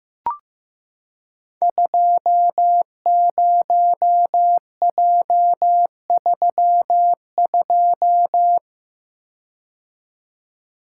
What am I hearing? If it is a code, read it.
Morse code, heard as a keyed tone, about 15 wpm, 700 Hz. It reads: 20J32
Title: Code Groups - 5 Letter:Numbers - 15wpm